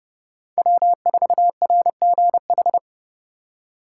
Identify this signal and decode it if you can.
Morse code, heard as a keyed tone, about 37 wpm, 700 Hz. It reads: W4RGH
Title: Speed Racing - USA 1x3 Call Signs 30wpm